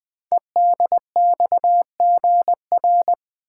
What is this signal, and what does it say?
Morse code, heard as a keyed tone, about 20 wpm, 700 Hz. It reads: EDXGR
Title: Code Groups - 5 Letters - 20wpm